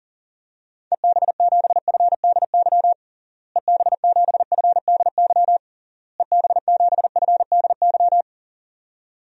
Morse code, heard as a keyed tone, about 37 wpm, 700 Hz. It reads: EB7FDY EB7FDY EB7FDY
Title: DXCC-100 Call Signs - 6 Characters - Repeated 3x - 40wpm